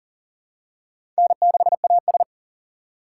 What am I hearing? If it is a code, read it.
Morse code, heard as a keyed tone, about 40 wpm, 700 Hz. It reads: N6AS
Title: USA Call Signs - 4 Characters - 2x Character Spacing - 40wpm